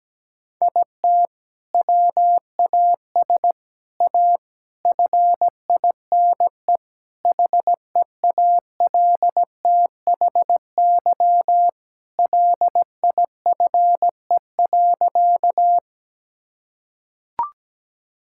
Morse code, heard as a keyed tone, about 17 wpm, 700 Hz. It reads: IT WAS A FINE HEALTHY LIFE.
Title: Sentences from Top 1000 Words - Review Part 5 - 17wpm